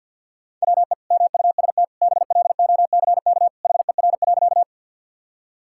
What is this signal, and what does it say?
Morse code, heard as a keyed tone, about 50 wpm, 700 Hz. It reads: WE MUST BLOCK HER.
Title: Sentences from Top 1600 Words 50wpm